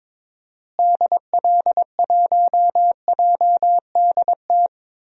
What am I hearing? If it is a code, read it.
Morse code, heard as a keyed tone, about 21 wpm, 700 Hz. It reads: DL1JDT
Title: DXCC-100 Call Signs - 6 Characters - Repeated 3x - 22wpm